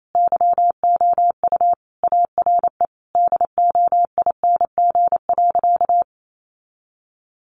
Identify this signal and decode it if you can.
Morse code, heard as a keyed tone, about 28 wpm, 700 Hz. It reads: YOU ARE DOING.
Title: Sentences from Top 1400 Words 28wpm